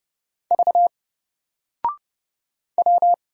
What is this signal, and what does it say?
Morse code, heard as a keyed tone, about 31 wpm, 700 Hz. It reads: V W
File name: Single-Letter-Number-30wpm